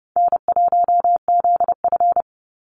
Morse code, heard as a keyed tone, about 30 wpm, 700 Hz. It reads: N1ZF
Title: NA SOTA Chaser Callsigns 30wpm